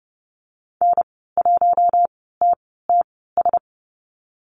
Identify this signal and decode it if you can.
Morse code, heard as a keyed tone, about 30 wpm, spaced as for 17 wpm, 700 Hz. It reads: N1TTS
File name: Lesson 14g - Number 2 - Call Signs 30wpm